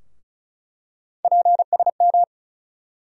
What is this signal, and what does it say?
Morse code, heard as a keyed tone, about 36 wpm, 700 Hz. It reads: PSM
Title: USA Call Sign Postfixes - 3 Letters 35wpm